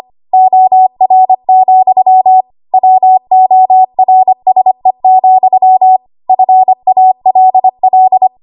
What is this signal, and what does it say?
Morse code, heard as a keyed tone, about 25 wpm, 750 Hz. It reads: OR, WORSE, FALL
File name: Archive / 25 WPM / 25WPM